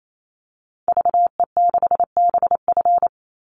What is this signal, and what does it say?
Morse code, heard as a keyed tone, about 28 wpm, 700 Hz. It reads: VE6BF
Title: Call Signs Repeated 2x - 28wpm